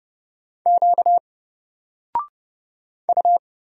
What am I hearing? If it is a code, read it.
Morse code, heard as a keyed tone, about 30 wpm, 700 Hz. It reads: Q U
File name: Single-Letter-Number-30wpm